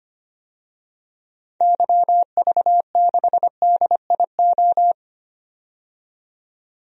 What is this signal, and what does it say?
Morse code, heard as a keyed tone, about 25 wpm, 700 Hz. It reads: YV6DIO
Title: DXCC-100 Call Signs - 6 Characters - 25wpm